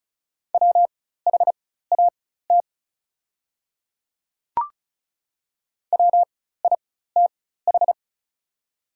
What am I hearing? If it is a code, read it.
Morse code, heard as a keyed tone, about 36 wpm, spaced as for 16 wpm, 700 Hz. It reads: WHAT WITH
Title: 4x Character Spacing - Top 100 Words - 35wpm